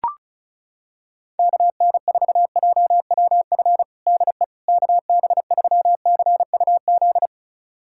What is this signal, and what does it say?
Morse code, heard as a keyed tone, about 36 wpm, 700 Hz. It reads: KN4JWF DE KB3CUZ
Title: USA Callsign DE Callsign 35wpm